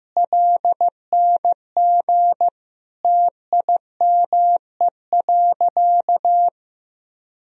Morse code, heard as a keyed tone, about 15 wpm, 700 Hz. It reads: LNG TIME.
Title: Sentences from Top 100 Words and Abbreviations 15wpm